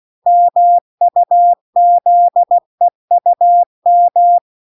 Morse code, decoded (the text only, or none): MUZEUM